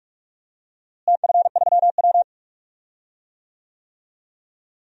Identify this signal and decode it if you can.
Morse code, heard as a keyed tone, about 47 wpm, 700 Hz. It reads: TF3W
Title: DXCC-100 Call Signs - 4 Characters - Repeated 3x - 45wpm